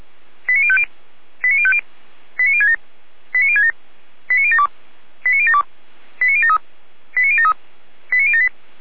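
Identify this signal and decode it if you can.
Five-tone selective call (ZVEI), heard as quick runs of five tones, tones of about 70 ms. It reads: 89050 89050 89067 89067 89072 89072 89073 89073 89078
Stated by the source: ZVEI five-tone call received by SDR, archived by Feuerwehr Orlamünde (fire brigade).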